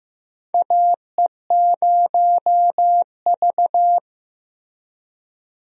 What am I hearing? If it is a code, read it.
Morse code, heard as a keyed tone, about 15 wpm, 700 Hz. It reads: AE0V
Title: USA Call Signs - 4 Characters - Repeated 3x - 15wpm